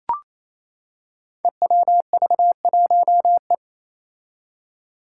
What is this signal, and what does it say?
Morse code, heard as a keyed tone, about 27 wpm, 700 Hz. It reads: EWV1E
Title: Five Challenging Letter-Number Combinations 28wpm